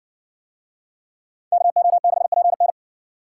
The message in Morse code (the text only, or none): KO6PN